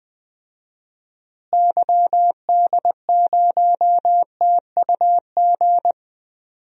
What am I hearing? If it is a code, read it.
Morse code, heard as a keyed tone, about 20 wpm, 700 Hz. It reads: YD0TUG
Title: DXCC-100 Call Signs - 6 Characters - Repeated 2x - 20wpm